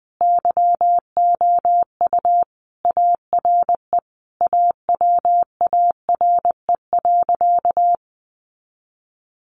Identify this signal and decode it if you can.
Morse code, heard as a keyed tone, about 20 wpm, 700 Hz. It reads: YOU ARE AWARE.
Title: Sentences from Top 1300 Words 20wpm